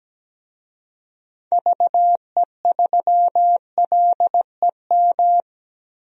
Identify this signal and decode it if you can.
Morse code, heard as a keyed tone, about 17 wpm, 700 Hz. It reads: VE3LEM
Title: DXCC-100 Call Signs - 6 Characters - 17wpm